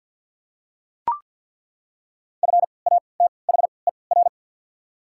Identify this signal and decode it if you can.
Morse code, heard as a keyed tone, about 48 wpm, spaced as for 28 wpm, 700 Hz. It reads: FATHER
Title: 3x Character Spacing - Top 200 Words - 50wpm